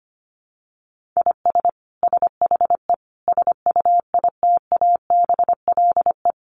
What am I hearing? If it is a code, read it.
Morse code, heard as a keyed tone, about 25 wpm, 700 Hz. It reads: IS SHE SUITABLE
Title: Sentences from Top 1400 Words 25wpm